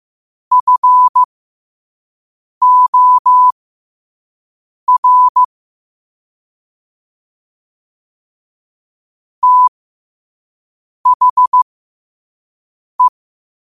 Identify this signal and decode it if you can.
Morse code, heard as a keyed tone, about 15 wpm, spaced as for 5 wpm, 1000 Hz. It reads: FOR THE
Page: Quotes for Fri, 15 Aug 2025 in Morse Code at 5 words per minute.